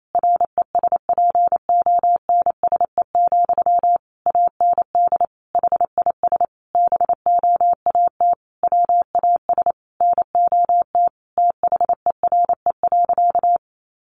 Morse code, heard as a keyed tone, about 28 wpm, 700 Hz. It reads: RESPONSE, AND HIS BOAT WAS NOT THERE.
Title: Sentences from Top 1400 Words 28wpm